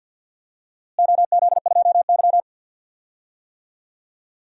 Morse code, heard as a keyed tone, about 49 wpm, 700 Hz. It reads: OZ2Y
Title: DXCC-100 Call Signs - 4 Characters - Repeated 2x - 50wpm